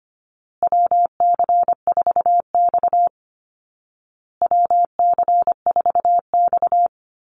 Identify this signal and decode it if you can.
Morse code, heard as a keyed tone, about 25 wpm, 700 Hz. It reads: WC4X WC4X
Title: Call Signs Repeated 2x - 25wpm